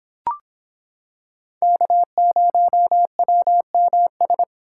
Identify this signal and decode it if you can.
Morse code, heard as a keyed tone, about 26 wpm, 700 Hz. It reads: K0WMS